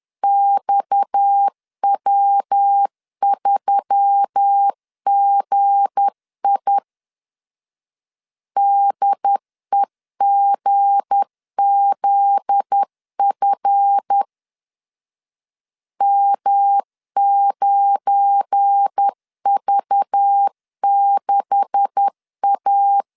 Morse code, heard as a keyed tone, about 10 wpm, 800 Hz. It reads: XW3GI DEGZF M9V6A
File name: LEZIONI / 9 286 10wpm 2735